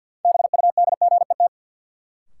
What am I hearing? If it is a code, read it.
Morse code, heard as a keyed tone, about 52 wpm, 700 Hz. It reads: BUDGET
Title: Sentences from Top 1200 Words - New Word Drill Alt - 50wpm